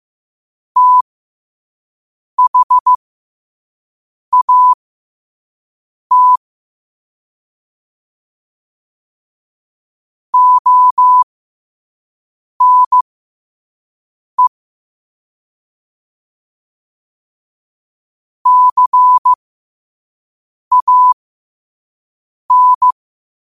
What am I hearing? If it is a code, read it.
Morse code, heard as a keyed tone, about 15 wpm, spaced as for 5 wpm, 1000 Hz. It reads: THAT ONE CAN